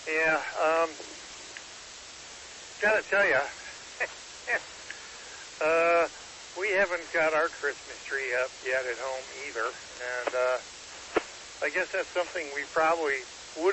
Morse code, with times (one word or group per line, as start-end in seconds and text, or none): none